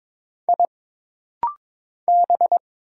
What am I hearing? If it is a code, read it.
Morse code, heard as a keyed tone, about 22 wpm, 700 Hz. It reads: I B
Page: This rapid-fire timing is geared to push your ICR (Instant Character Recognition) skills.